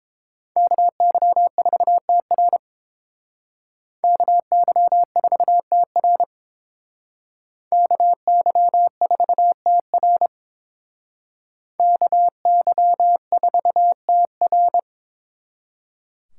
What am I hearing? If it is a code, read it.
Morse code, heard as a keyed tone, about 29 wpm, 700 Hz. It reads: KY4TR KY4TR KY4TR KY4TR